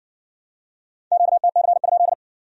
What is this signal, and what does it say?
Morse code, heard as a keyed tone, about 58 wpm, 700 Hz. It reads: QTC?